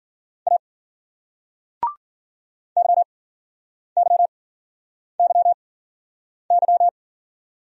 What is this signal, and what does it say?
Morse code, heard as a keyed tone, about 51 wpm, spaced as for 9 wpm, 700 Hz. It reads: A YYYY